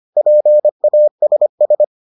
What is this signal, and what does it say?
Morse code, heard as a keyed tone, about 25 wpm, 600 Hz. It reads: PASS